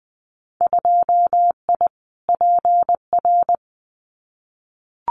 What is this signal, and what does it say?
Morse code, heard as a keyed tone, about 20 wpm, 700 Hz. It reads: 2I PR
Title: Winter Field Day Exchanges Encore 20wpm